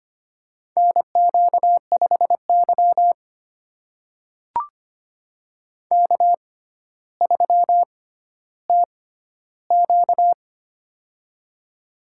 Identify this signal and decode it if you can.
Morse code, heard as a keyed tone, about 25 wpm, spaced as for 13 wpm, 700 Hz. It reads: NQ5Y K3TQ
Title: USA Call Signs - 6x Character Spacing - 25wpm